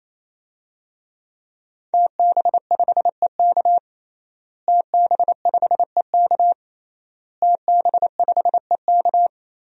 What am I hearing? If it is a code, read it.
Morse code, heard as a keyed tone, about 28 wpm, 700 Hz. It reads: TB5EK TB5EK TB5EK
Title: DXCC-100 Call Signs - 5 Characters - Repeated 3x - 28wpm